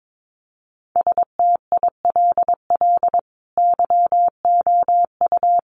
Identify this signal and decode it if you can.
Morse code, heard as a keyed tone, about 22 wpm, 700 Hz. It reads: STILL YOU
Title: Sentences from Top 600 Words 22wpm